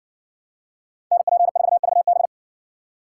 Morse code, heard as a keyed tone, about 60 wpm, 700 Hz. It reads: NP4VB